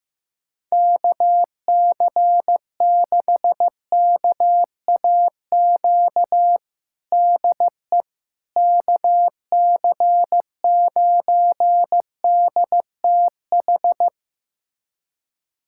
Morse code, heard as a keyed tone, about 16 wpm, 700 Hz. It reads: KC6KAQ DE KC9DTH
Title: USA Callsign DE Callsign - 6 Characters - 15wpm